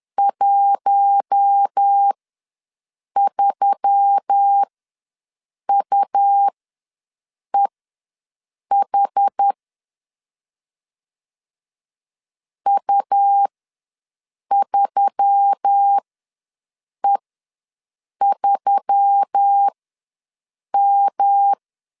Morse code, heard as a keyed tone, about 10 wpm, spaced as for 6 wpm, 800 Hz. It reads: 13UEH U3E3M
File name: LEZIONI / 5 LBVC / 5 LB ws 10wpm 3616